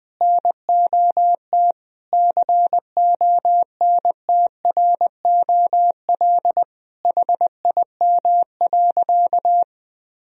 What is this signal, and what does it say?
Morse code, heard as a keyed tone, about 21 wpm, 700 Hz. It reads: NOT CONTROL HIM.
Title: Sentences from Top 1000 Words - Review Part 3 - 20wpm